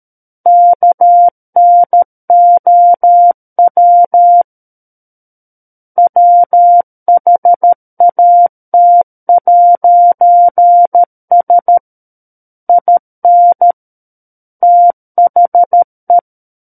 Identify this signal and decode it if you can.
Morse code, heard as a keyed tone, about 13 wpm, 700 Hz. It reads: KNOW WHAT'S IN THE
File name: Book One / 04-Chapter 4 - 13 WPM